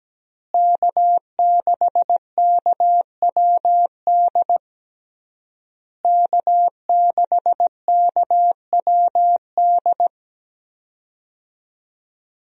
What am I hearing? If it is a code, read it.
Morse code, heard as a keyed tone, about 17 wpm, 700 Hz. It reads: K6KWD K6KWD
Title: USA Call Signs - 5 Characters - Repeated 3x - 17wpm